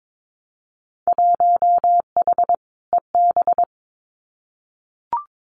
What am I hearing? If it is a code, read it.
Morse code, heard as a keyed tone, about 22 wpm, 700 Hz. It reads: 1H EB
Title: Winter Field Day Exchanges 22wpm